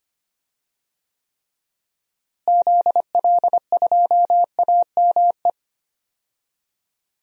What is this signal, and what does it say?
Morse code, heard as a keyed tone, about 25 wpm, 700 Hz. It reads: ZL2AME
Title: DXCC-100 Call Signs - 6 Characters - Repeated 2x - 25wpm